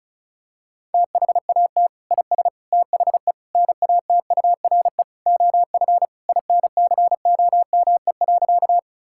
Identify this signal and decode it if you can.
Morse code, heard as a keyed tone, about 36 wpm, 700 Hz. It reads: THAT IS THE NATURE OF INCOME.
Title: Sentences from Top 1000 Words - Review Part 5 - 35wpm